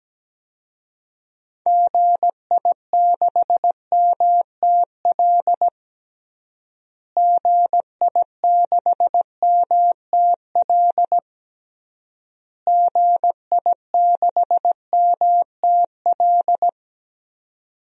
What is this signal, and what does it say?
Morse code, heard as a keyed tone, about 17 wpm, 700 Hz. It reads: GI6MTL GI6MTL GI6MTL
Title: DXCC-100 Call Signs - 6 Characters - Repeated 3x - 17wpm